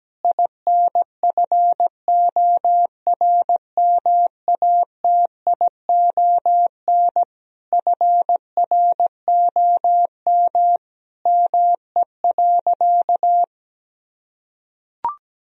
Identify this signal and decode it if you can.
Morse code, heard as a keyed tone, about 17 wpm, 700 Hz. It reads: INFORMATION FROM ME.
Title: Sentences from Top 1000 Words - Review Part 3 - 17wpm